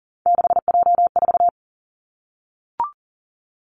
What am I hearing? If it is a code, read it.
Morse code, heard as a keyed tone, about 38 wpm, 700 Hz. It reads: 6J4
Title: Three Challenging Letter-Number Combinations 40wpm